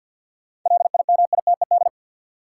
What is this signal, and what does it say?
Morse code, heard as a keyed tone, about 53 wpm, 700 Hz. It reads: LIMITED
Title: Sentences from Top 900 Words - New Word Drill - 50wpm